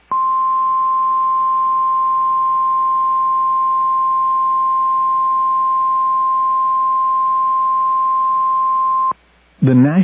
Emergency Alert System (SAME) tones (the 1050 Hz attention tone). It is heard from 0.8 to 8.9 s.